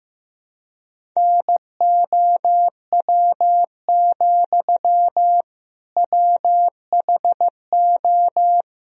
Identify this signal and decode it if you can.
Morse code, heard as a keyed tone, about 15 wpm, 700 Hz. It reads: NOW, WHO
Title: Sentences from Top 1000 Words - Review Part 1 - 15wpm